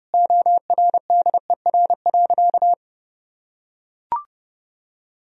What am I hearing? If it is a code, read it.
Morse code, heard as a keyed tone, about 30 wpm, 700 Hz. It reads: ORDER.
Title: Sentences from Top 1000 Words - Review Part 2 - 30wpm